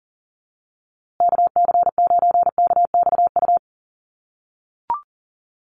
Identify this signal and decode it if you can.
Morse code, heard as a keyed tone, about 40 wpm, 700 Hz. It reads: KC9KXU
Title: USA Call Signs Repeated 3x 40wpm